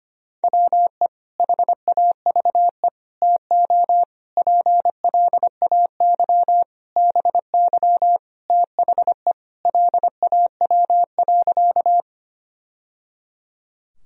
Morse code, heard as a keyed tone, about 25 wpm, 700 Hz. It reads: WE HAVE TO PLAY BY THE LAW.
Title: Sentences from Top 1000 Words - Review Part 3 - 25wpm